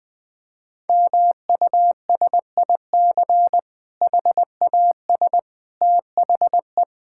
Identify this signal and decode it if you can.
Morse code, heard as a keyed tone, about 20 wpm, 700 Hz. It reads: MUSIC HAS THE
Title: Difficult Sentences with Hints - 20wpm